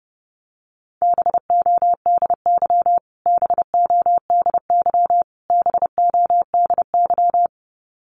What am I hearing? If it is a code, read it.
Morse code, heard as a keyed tone, about 30 wpm, 700 Hz. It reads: BODY BODY BODY
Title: Top 500 Words Repeated 30wpm